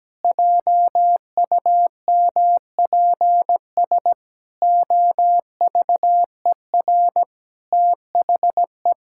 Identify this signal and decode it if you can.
Morse code, heard as a keyed tone, about 17 wpm, 700 Hz. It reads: JUMPS OVER THE
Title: Pangram Challenge v2 - 17wpm